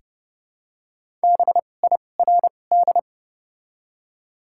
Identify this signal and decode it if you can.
Morse code, heard as a keyed tone, about 31 wpm, spaced as for 22 wpm, 690 Hz. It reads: BIRD
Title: 2x Character Spacing - Top 500 Words - 30wpm